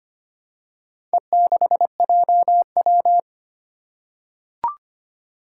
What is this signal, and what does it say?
Morse code, heard as a keyed tone, about 25 wpm, 700 Hz. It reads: E6JW
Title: Four Challenging Letter-Number Combinations 25wpm